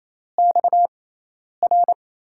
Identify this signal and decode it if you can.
Morse code, heard as a keyed tone, about 28 wpm, spaced as for 10 wpm, 700 Hz. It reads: XR
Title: USA Call Signs - 6 Characters - 6x Character Spacing - 28wpm